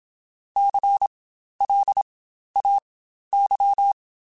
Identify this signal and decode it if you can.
Morse code, heard as a keyed tone, about 27 wpm, spaced as for 12 wpm, 800 Hz. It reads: CLAY